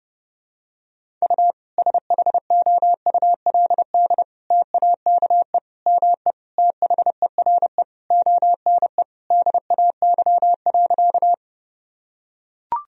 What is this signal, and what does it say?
Morse code, heard as a keyed tone, about 30 wpm, 700 Hz. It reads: U SHOULD TAKE ME THERE ONE DAY.
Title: Sentences from Top 300 Words and Abbreviations 30wpm